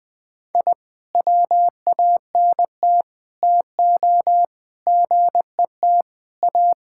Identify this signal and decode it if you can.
Morse code, heard as a keyed tone, about 20 wpm, 700 Hz. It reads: I WANT TO GET A
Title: Sentences with 7-Word Phrases - Spoken Beginning - 20wpm